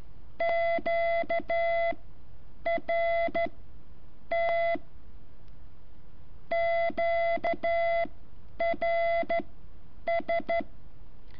Morse code, heard as a keyed tone, about 12 wpm, spaced as for 8 wpm, 680 Hz. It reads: Q R T QRS
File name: curso / ejercicio 14